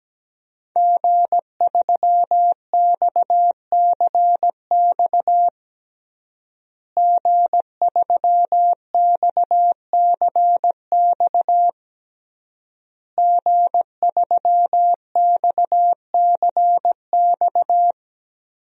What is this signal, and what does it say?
Morse code, heard as a keyed tone, about 17 wpm, 700 Hz. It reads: G3XCX G3XCX G3XCX
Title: DXCC-100 Call Signs - 5 Characters - Repeated 3x - 17wpm